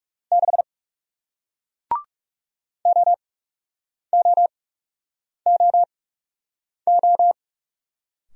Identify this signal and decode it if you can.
Morse code, heard as a keyed tone, about 40 wpm, spaced as for 8 wpm, 700 Hz. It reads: 6 OOOO